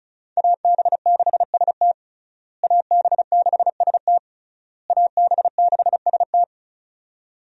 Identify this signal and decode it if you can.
Morse code, heard as a keyed tone, about 36 wpm, 700 Hz. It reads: AB6ST AB6ST AB6ST
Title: USA Call Signs - 5 Characters - Repeated 3x - 35wpm